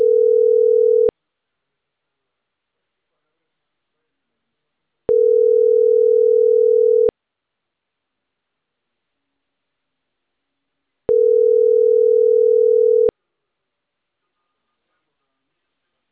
Ringback tone (North America), 440 + 480 Hz.